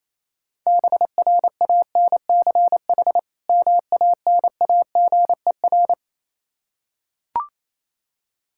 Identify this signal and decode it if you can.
Morse code, heard as a keyed tone, about 27 wpm, 700 Hz. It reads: BRANCH MANAGER
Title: Practice QSO - Exchange 3 - More Info - 28wpm